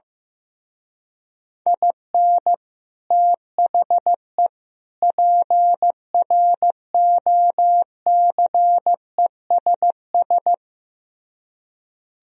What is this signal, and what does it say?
Morse code, heard as a keyed tone, about 15 wpm, 700 Hz. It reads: IN THE PROCESS
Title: Speed-Racing - Sets of 3 Words 15wpm